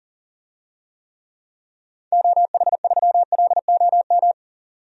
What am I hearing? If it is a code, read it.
Morse code, heard as a keyed tone, about 37 wpm, 700 Hz. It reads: OH3LOM
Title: DXCC-100 Call Signs - 6 Characters - 40wpm